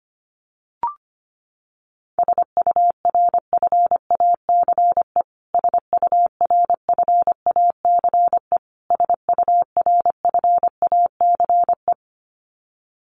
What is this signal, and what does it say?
Morse code, heard as a keyed tone, about 25 wpm, 700 Hz. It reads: SURFACE SURFACE SURFACE
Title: Top 500 Words Repeated 25wpm